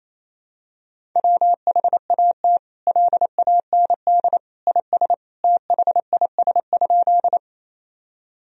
Morse code, heard as a keyed tone, about 28 wpm, 700 Hz. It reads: WHAT LAND IS THIS?
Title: Sentences from Top 1000 Words - Review Part 1 - 28wpm